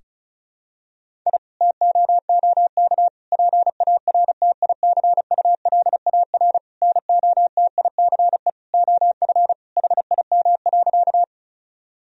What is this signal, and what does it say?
Morse code, heard as a keyed tone, about 34 wpm, 700 Hz. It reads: I TOOK PARTICULAR NOTICE OF HIM.
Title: Sentences from Top 1000 Words - Review Part 4 - 35wpm